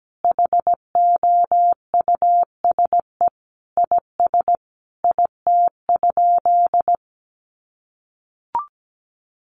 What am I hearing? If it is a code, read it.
Morse code, heard as a keyed tone, about 17 wpm, 700 Hz. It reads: HOUSE IS IT?
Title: Sentences from Top 200 Words 17wpm